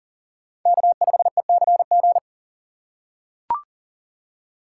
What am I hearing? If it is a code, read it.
Morse code, heard as a keyed tone, about 40 wpm, 700 Hz. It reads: K5ECG
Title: USA Call Signs - 5 Characters - 8x Character Spacing - 40wpm